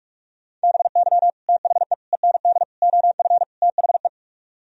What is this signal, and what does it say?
Morse code, heard as a keyed tone, about 47 wpm, 700 Hz. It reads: BY THE END OF THE